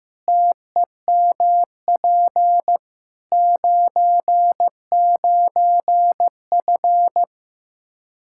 Morse code, heard as a keyed tone, about 16 wpm, 700 Hz. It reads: TEMP 99F
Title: Practice QSO - Exchange 2 - Info - 15wpm